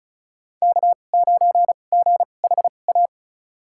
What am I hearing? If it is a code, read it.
Morse code, heard as a keyed tone, about 36 wpm, spaced as for 25 wpm, 700 Hz. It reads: K9GHA
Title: USA Call Signs - 5 Characters - 2x Character Spacing - 35wpm